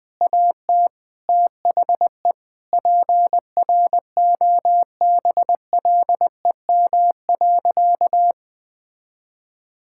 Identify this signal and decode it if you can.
Morse code, heard as a keyed tone, about 20 wpm, 700 Hz. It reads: AT THE PROBLEM.
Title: Sentences with 7-Word Phrases - 20wpm